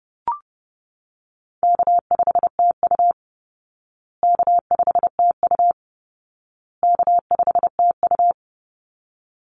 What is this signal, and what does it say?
Morse code, heard as a keyed tone, about 28 wpm, 700 Hz. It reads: K5TU K5TU K5TU
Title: Call Signs Repeated 2x - 30wpm